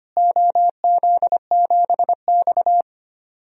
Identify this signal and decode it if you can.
Morse code, heard as a keyed tone, about 25 wpm, 700 Hz. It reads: OZ7X
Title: DXCC-100 Call Signs - 4 Characters - Repeated 3x - 25wpm